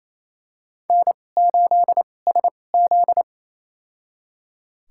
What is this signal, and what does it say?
Morse code, heard as a keyed tone, about 29 wpm, spaced as for 21 wpm, 700 Hz. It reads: N8SZ
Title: USA Call Signs - 4 Characters - 2x Character Spacing - 28wpm